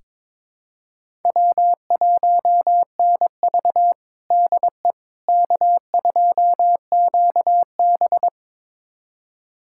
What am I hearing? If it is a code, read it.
Morse code, heard as a keyed tone, about 22 wpm, 700 Hz. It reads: W1NV DE K2QB
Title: USA Callsign DE Callsign - 4 Characters - 22wpm